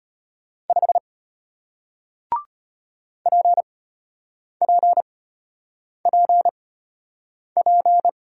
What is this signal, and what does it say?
Morse code, heard as a keyed tone, about 34 wpm, spaced as for 8 wpm, 690 Hz. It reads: 5 PPPP